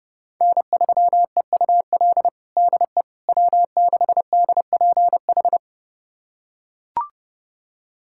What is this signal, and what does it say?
Morse code, heard as a keyed tone, about 30 wpm, 700 Hz. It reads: N3EUL DE W6DPH
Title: USA Callsign DE Callsign - 5 Characters - 30wpm